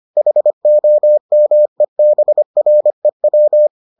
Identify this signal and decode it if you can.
Morse code, heard as a keyed tone, about 25 wpm, 600 Hz. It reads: HOMEBREW